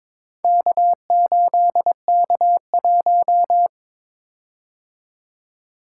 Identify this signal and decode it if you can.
Morse code, heard as a keyed tone, about 22 wpm, 700 Hz. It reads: K8K1
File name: practice / 22wpm / Four Letter-Number Combinations - 22wpm